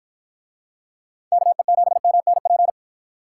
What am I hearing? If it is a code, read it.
Morse code, heard as a keyed tone, about 50 wpm, 700 Hz. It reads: KE7MNP